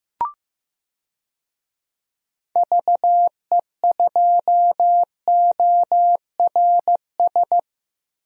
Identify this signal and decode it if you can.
Morse code, heard as a keyed tone, about 15 wpm, 700 Hz. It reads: VE2ORS
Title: DXCC-100 Call Signs - 6 Characters - Repeated 3x - 15wpm